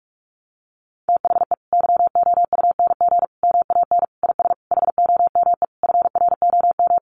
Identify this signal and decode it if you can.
Morse code, heard as a keyed tone, about 44 wpm, 700 Hz. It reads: THE YOUNG MAN IS HOME FROM
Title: Sentences from Top 1300 Words 45wpm